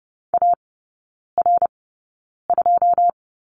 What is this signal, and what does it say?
Morse code, heard as a keyed tone, about 30 wpm, spaced as for 9 wpm, 700 Hz. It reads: AR2
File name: course / 30:10wpm / 087 Lesson 15e - Number 5 - Three Characters 30wpm